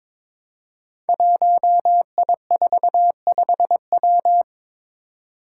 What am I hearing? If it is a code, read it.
Morse code, heard as a keyed tone, about 22 wpm, 700 Hz. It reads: 1I45W
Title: Five Challenging Letter-Number Combinations 22wpm